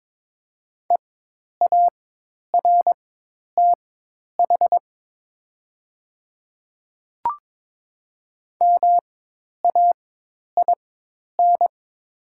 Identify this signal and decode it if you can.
Morse code, heard as a keyed tone, about 22 wpm, spaced as for 10 wpm, 700 Hz. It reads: EARTH MAIN
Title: 4x Character Spacing - Top 300 Words - 22wpm